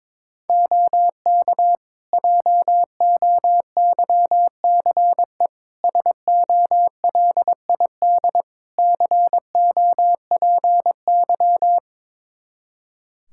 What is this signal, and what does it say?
Morse code, heard as a keyed tone, about 22 wpm, 700 Hz. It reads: OK JOYCE SOLID COPY
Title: Practice QSO - Exchange 3 - More Info - 22wpm